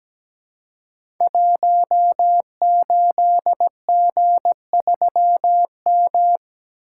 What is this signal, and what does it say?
Morse code, heard as a keyed tone, about 17 wpm, 700 Hz. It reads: 18G3M
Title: Code Groups - 5 Letter:Numbers - 17wpm